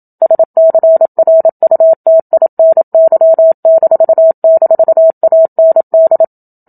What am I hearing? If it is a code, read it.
Morse code, heard as a keyed tone, about 27 wpm, 650 Hz. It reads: SCRUTINY--AND
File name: Book Two / 01-Chapter 1 - 27 WPM